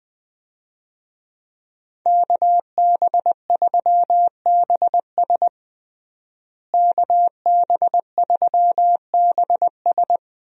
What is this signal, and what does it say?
Morse code, heard as a keyed tone, about 20 wpm, 700 Hz. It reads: KB3BS KB3BS
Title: USA Call Signs - 5 Characters - Repeated 3x - 20wpm